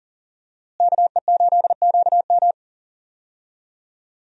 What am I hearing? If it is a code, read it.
Morse code, heard as a keyed tone, about 40 wpm, 700 Hz. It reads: KE8QM
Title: USA Call Signs - 5 Characters - Repeated 3x - 40wpm